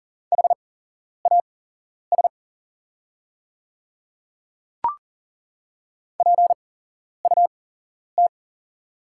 Morse code, heard as a keyed tone, about 40 wpm, spaced as for 11 wpm, 690 Hz. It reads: HAS PUT
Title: 8x Character Spacing - Top 200 Words - 40wpm